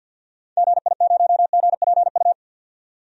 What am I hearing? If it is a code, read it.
Morse code, heard as a keyed tone, about 47 wpm, 700 Hz. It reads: GI0GPU
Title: DXCC-100 Call Signs - 6 Characters - 50wpm